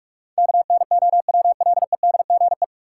Morse code, heard as a keyed tone, about 46 wpm, 700 Hz. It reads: KNOWLEDGE
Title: Sentences from Top 600 Words - New Word Drill - 45wpm